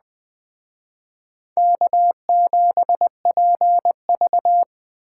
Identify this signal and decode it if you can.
Morse code, heard as a keyed tone, about 20 wpm, 700 Hz. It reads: K7PV
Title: Speed Racing - USA Call Signs 17wpm